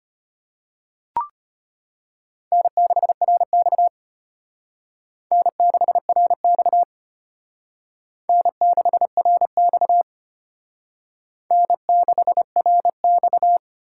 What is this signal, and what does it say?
Morse code, heard as a keyed tone, about 32 wpm, 700 Hz. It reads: N6RX N6RX N6RX N6RX